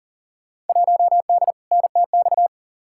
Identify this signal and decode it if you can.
Morse code, heard as a keyed tone, about 38 wpm, 700 Hz. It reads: 1D NTX
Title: Field Day Exchanges - 40wpm